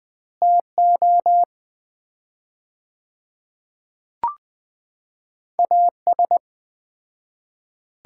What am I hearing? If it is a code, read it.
Morse code, heard as a keyed tone, about 20 wpm, 700 Hz. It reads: TO AS
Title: 4x Word-Spacing - Sets of 4 Words - 20wpm